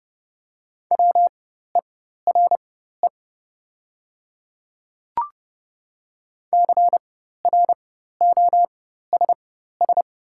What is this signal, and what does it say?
Morse code, heard as a keyed tone, about 30 wpm, spaced as for 14 wpm, 700 Hz. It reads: WERE CROSS
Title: 4x Character Spacing - Top 300 Words - 30wpm